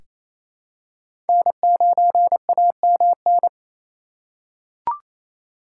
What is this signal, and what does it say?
Morse code, heard as a keyed tone, about 28 wpm, 700 Hz. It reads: N9AMN
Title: USA Call Signs - 5 Characters - 2x Character Spacing - 28wpm